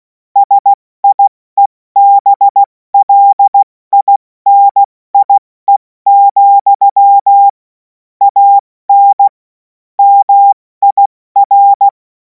Morse code, heard as a keyed tone, about 16 wpm, 800 Hz. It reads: SIEBLINIE, AN MIR
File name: Morsekurs / Witze / Witz 10. Februar 2026 70 F80